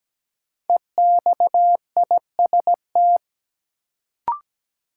Morse code, heard as a keyed tone, about 17 wpm, 700 Hz. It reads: EXIST
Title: Sentences from Top 1100 Words - New Word Drill Alt - 17wpm